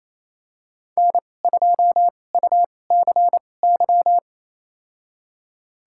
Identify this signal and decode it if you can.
Morse code, heard as a keyed tone, about 28 wpm, spaced as for 20 wpm, 690 Hz. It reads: N2UCY
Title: USA Call Signs - 2x Character Spacing - 28wpm